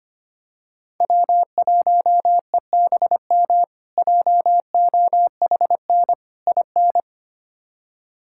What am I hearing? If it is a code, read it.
Morse code, heard as a keyed tone, about 25 wpm, 700 Hz. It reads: W1EBM JOHN IN
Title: K1USN SST QSO Practice - 25wpm